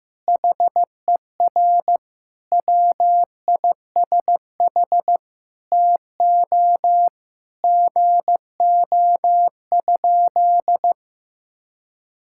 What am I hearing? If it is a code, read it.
Morse code, heard as a keyed tone, about 15 wpm, 700 Hz. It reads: HER WISH TO GO?
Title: Sentences from Top 1600 Words 15wpm